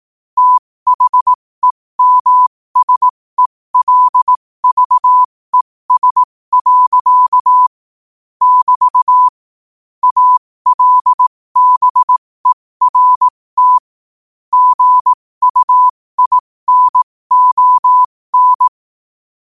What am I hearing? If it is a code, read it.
Morse code, heard as a keyed tone, about 18 wpm, spaced as for 15 wpm, 1000 Hz. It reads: THEMSELVES. <BT> ALBERT GUINON
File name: qotdfiles / 15wpm